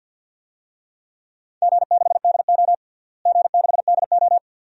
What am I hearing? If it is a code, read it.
Morse code, heard as a keyed tone, about 50 wpm, 700 Hz. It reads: G6DO G6DO
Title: DXCC-100 Call Signs - 4 Characters - Repeated 2x - 50wpm